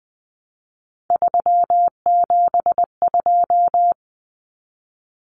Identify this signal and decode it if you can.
Morse code, heard as a keyed tone, about 20 wpm, 700 Hz. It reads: 372
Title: Common Cut Numbers - 3-Digit 20wpm